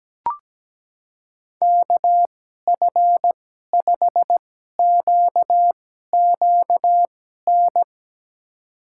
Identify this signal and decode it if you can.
Morse code, heard as a keyed tone, about 17 wpm, spaced as for 12 wpm, 700 Hz. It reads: KF5QQN
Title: USA Call Signs - 6 Characters - 2x Character Spacing - 17wpm